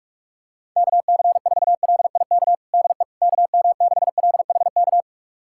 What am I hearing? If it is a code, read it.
Morse code, heard as a keyed tone, about 46 wpm, 700 Hz. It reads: KC4LIK DE KM6LHK
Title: USA Callsign DE Callsign - 6 Characters - 45wpm